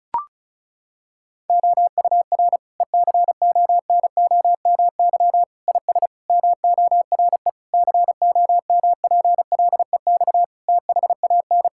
Morse code, heard as a keyed tone, about 34 wpm, 700 Hz. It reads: OUR ECONOMY IS MORE COMPLEX THAN
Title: Sentences from Top 1000 Words - Review Part 6 - 35wpm